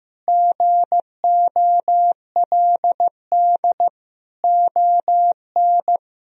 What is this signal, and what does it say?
Morse code, heard as a keyed tone, about 15 wpm, 700 Hz. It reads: GOLD ON
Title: Sentences from Top 1000 Words - Review Part 4 - 15wpm